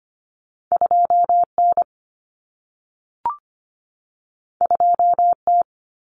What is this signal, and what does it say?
Morse code, heard as a keyed tone, about 25 wpm, 700 Hz. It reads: 2N 2T
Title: Common Cut Numbers - 2-Digit 25wpm